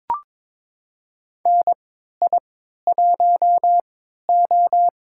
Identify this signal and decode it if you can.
Morse code, heard as a keyed tone, about 22 wpm, spaced as for 13 wpm, 700 Hz. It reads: NI1O
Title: USA Call Signs - 4 Characters - 3x Character Spacing - 22wpm